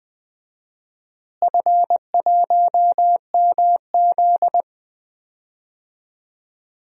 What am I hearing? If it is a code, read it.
Morse code, heard as a keyed tone, about 19 wpm, 690 Hz. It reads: F1MZ
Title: DXCC-100 Call Signs - 4 Characters - Repeated 3x - 20wpm